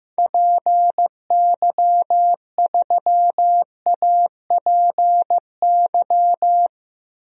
Morse code, heard as a keyed tone, about 15 wpm, 700 Hz. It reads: PY3APY
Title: DXCC-100 Call Signs - 6 Characters - 15wpm